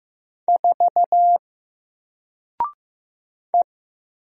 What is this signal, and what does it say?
Morse code, heard as a keyed tone, about 15 wpm, 700 Hz. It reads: 4 E
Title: Single Letter-Number Alt 15wpm